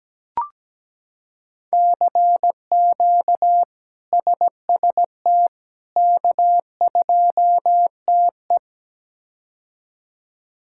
Morse code, heard as a keyed tone, about 17 wpm, 700 Hz. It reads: CQ SST K2TE
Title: K1USN SST QSO Practice - 17wpm